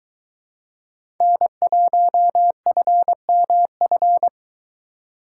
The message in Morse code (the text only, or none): N1FMF